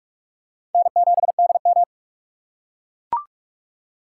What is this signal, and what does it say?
Morse code, heard as a keyed tone, about 44 wpm, 700 Hz. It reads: N7DM
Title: USA Call Signs - 4 Characters - 3x Character Spacing - 45wpm